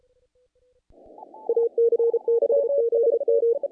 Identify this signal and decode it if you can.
Morse code, heard as a keyed tone, about 33 wpm, 470 Hz. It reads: AC6ZM
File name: contests / naqp-cw-k5tr-op-k5pi- -1800